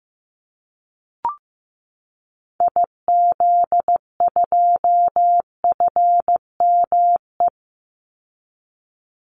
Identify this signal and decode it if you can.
Morse code, heard as a keyed tone, about 15 wpm, 700 Hz. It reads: IZ2FME
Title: Call Signs Repeated 3x - 15wpm